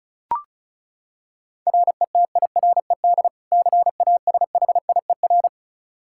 Morse code, heard as a keyed tone, about 34 wpm, 700 Hz. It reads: RETIRED CASHIER
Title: Practice QSO - Exchange 3 - More Info - 35wpm